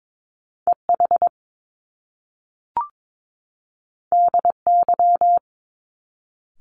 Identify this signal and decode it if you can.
Morse code, heard as a keyed tone, about 22 wpm, 700 Hz. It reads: EH DY
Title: Top Two Letter Combinations in Words 22wpm